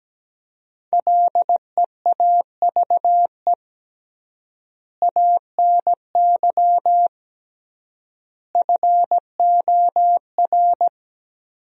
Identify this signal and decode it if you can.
Morse code, heard as a keyed tone, about 17 wpm, 700 Hz. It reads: LEAVE ANY FOR
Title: 3x Word-Spacing - Sentences from Top 400 Words - 17wpm